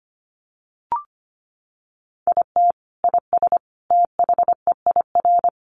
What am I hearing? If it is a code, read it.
Morse code, heard as a keyed tone, about 25 wpm, 700 Hz. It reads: IT IS THEIR
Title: Sentences from Top 1000 Words 25wpm